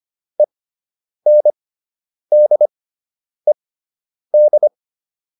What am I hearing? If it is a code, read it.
Morse code, heard as a keyed tone, about 25 wpm, spaced as for 9 wpm, 600 Hz. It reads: ENDED
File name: ED5 09